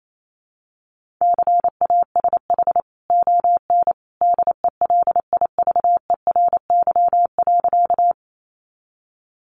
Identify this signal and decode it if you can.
Morse code, heard as a keyed tone, about 28 wpm, 700 Hz. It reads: CASH ON DELIVERY.
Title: Sentences from Top 1500 Words 28wpm